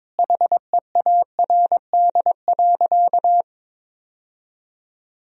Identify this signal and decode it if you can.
Morse code, heard as a keyed tone, about 22 wpm, 700 Hz. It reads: HEARD.
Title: Sentences from Top 1000 Words - Review Part 4 - 22wpm